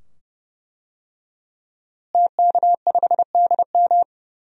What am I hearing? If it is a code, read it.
Morse code, heard as a keyed tone, about 30 wpm, 700 Hz. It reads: TK5DM
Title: DXCC-100 Call Signs - 5 Characters - Repeated 2x - 30wpm